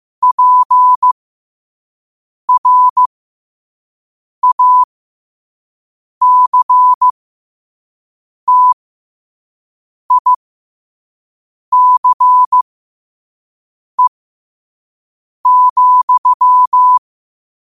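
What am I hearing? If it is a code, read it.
Morse code, heard as a keyed tone, about 15 wpm, spaced as for 5 wpm, 1000 Hz. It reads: PRACTICE,
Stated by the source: Morse quote of the day at 5 wpm.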